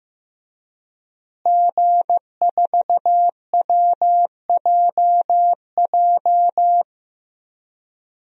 Morse code, heard as a keyed tone, about 14 wpm, 700 Hz. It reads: G4WJJ
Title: DXCC-100 Call Signs - 5 Characters - Repeated 3x - 15wpm